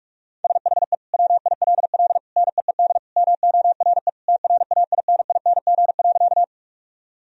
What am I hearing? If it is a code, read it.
Morse code, heard as a keyed tone, about 46 wpm, 700 Hz. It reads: SHE WILL NEED MORE TRAINING.
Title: Sentences from Top 1000 Words - Review Part 4 - 45wpm